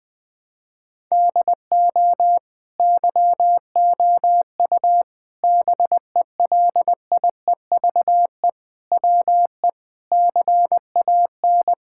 Morse code, heard as a keyed tone, about 20 wpm, 700 Hz. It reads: DO YOU BELIEVE WE CAN
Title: Sentences from Top 1000 Words - Review Part 6 - 20wpm